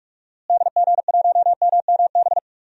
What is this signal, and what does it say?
Morse code, heard as a keyed tone, about 46 wpm, 700 Hz. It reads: DG1MMB
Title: DXCC-100 Call Signs - 6 Characters - Repeated 2x - 45wpm